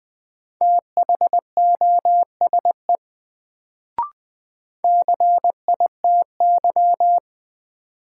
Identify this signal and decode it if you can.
Morse code, heard as a keyed tone, about 20 wpm, 700 Hz. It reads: THOSE CITY
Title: Sentences from Top 300 Words - New Word Drill - 20wpm